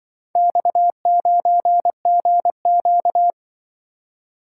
Morse code, heard as a keyed tone, about 24 wpm, 700 Hz. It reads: X9GQ
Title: Speed-Racing - Four Letter-Number Combinations - 20wpm